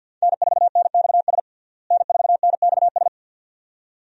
Morse code, heard as a keyed tone, about 48 wpm, 700 Hz. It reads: N4NXS N4NXS
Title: USA Call Signs - 5 Characters - Repeated 2x - 50wpm